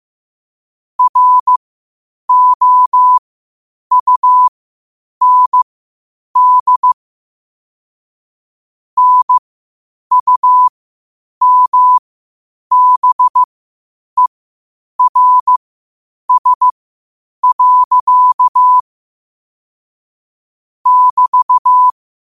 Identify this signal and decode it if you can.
Morse code, heard as a keyed tone, about 15 wpm, spaced as for 8 wpm, 1000 Hz. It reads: ROUND NUMBERS. <BT>
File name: qotdfiles / 8wpm